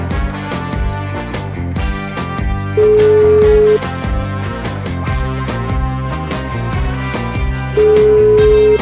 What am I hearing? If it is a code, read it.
Ringback tone (Europe, 425 Hz), with noise.